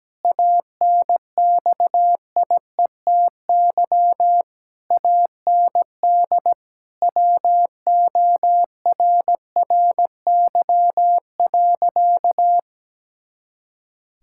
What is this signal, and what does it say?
Morse code, heard as a keyed tone, about 17 wpm, 700 Hz. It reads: ANXIETY AND WORRY.
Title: Pangram Challenge v2 - 17wpm